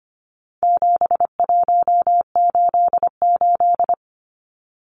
Morse code, heard as a keyed tone, about 25 wpm, 700 Hz. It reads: 7188
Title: Numbers - 4-Digit 25wpm